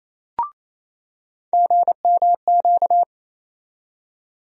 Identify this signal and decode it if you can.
Morse code, heard as a keyed tone, about 28 wpm, 700 Hz. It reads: GMQ
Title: USA Call Sign Postfixes - 3 Letters 28wpm